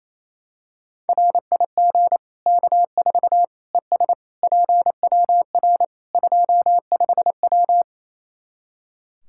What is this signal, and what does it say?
Morse code, heard as a keyed tone, about 28 wpm, 700 Hz. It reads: RIG K4 ES PWR 25W
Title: Practice QSO - Exchange 2 - Info - 28wpm